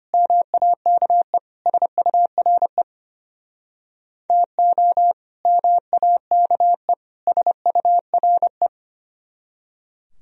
Morse code, heard as a keyed tone, about 32 wpm, 700 Hz. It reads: MAKE SURE TO MAKE SURE